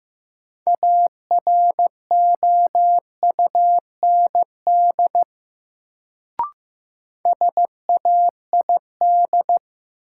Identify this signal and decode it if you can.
Morse code, heard as a keyed tone, about 15 wpm, 700 Hz. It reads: AROUND SAID
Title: Sets of 2 Words - New Word Drill - 15wpm